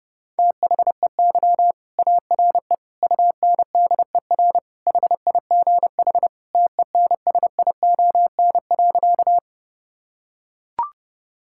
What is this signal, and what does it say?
Morse code, heard as a keyed tone, about 30 wpm, 700 Hz. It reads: THEY ARE UNDER HIGH TENSION.
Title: Sentences from Top 1600 Words 30wpm